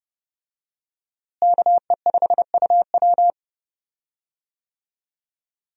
Morse code, heard as a keyed tone, about 30 wpm, 700 Hz. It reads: KE5UW
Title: USA Call Signs - 5 Characters - Repeated 2x - 30wpm